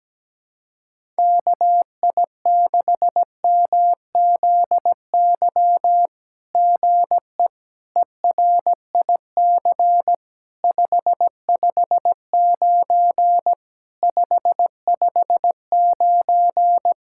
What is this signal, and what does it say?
Morse code, heard as a keyed tone, about 17 wpm, 700 Hz. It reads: KI6MZY GE ERIC 559 559
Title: SOTA QSO Practice 17wpm